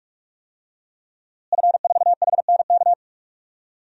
Morse code, heard as a keyed tone, about 44 wpm, 700 Hz. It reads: F4HNK
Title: DXCC-100 Call Signs - 5 Characters - Repeated 2x - 45wpm